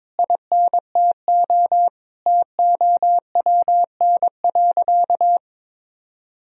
Morse code, heard as a keyed tone, about 22 wpm, 700 Hz. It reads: INTO TOWN.
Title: Sentences from Top 1000 Words - Review Part 2 - 22wpm